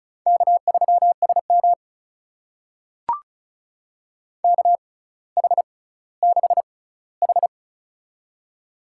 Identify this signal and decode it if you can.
Morse code, heard as a keyed tone, about 36 wpm, spaced as for 18 wpm, 700 Hz. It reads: K3SM KH6H
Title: USA Call Signs - 4 Characters - 6x Character Spacing - 35wpm